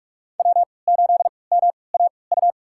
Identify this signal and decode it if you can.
Morse code, heard as a keyed tone, about 43 wpm, spaced as for 25 wpm, 700 Hz. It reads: W8MAU
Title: USA Call Signs - 3x Character Spacing - 45wpm